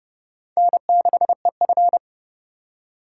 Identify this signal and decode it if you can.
Morse code, heard as a keyed tone, about 30 wpm, 700 Hz. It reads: N6EF
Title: USA Call Signs - 4 Characters - 2x Character Spacing - 30wpm